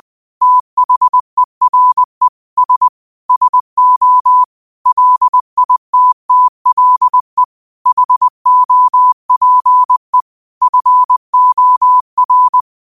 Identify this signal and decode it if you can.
Morse code, heard as a keyed tone, about 20 wpm, 1000 Hz. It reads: THERE S SO LITTLE HOPE FOR